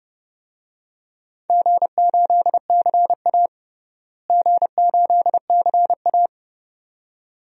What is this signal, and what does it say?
Morse code, heard as a keyed tone, about 30 wpm, 700 Hz. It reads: G8CA G8CA
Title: DXCC-100 Call Signs - 4 Characters - Repeated 2x - 30wpm